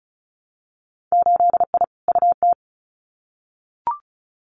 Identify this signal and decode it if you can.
Morse code, heard as a keyed tone, about 36 wpm, 700 Hz. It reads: 8I UT
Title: Winter Field Day Exchanges Encore 35wpm